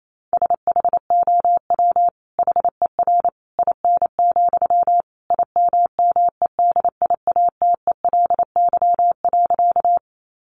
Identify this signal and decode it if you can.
Morse code, heard as a keyed tone, about 28 wpm, 700 Hz. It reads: SHOW HER IN, IMMEDIATELY.
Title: Sentences from Top 1100 Words 28wpm